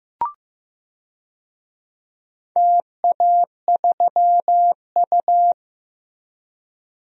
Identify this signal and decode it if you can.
Morse code, heard as a keyed tone, about 15 wpm, 700 Hz. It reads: TA3U
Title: DXCC-100 Call Signs - 4 Characters - Repeated 2x - 15wpm